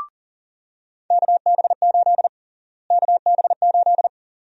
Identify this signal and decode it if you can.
Morse code, heard as a keyed tone, about 40 wpm, 700 Hz. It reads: KB8 KB8
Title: USA Call Sign Prefixes - Repeated 2x - 40wpm